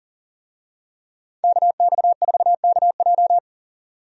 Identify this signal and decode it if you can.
Morse code, heard as a keyed tone, about 40 wpm, 700 Hz. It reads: KX4KJ
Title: USA Call Signs - 5 Characters - 40wpm